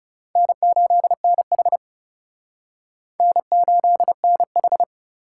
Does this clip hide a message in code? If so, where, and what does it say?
Yes: Morse code from 0.4 to 4.8 s, heard as a keyed tone, about 40 wpm, 700 Hz. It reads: N8NH N8NH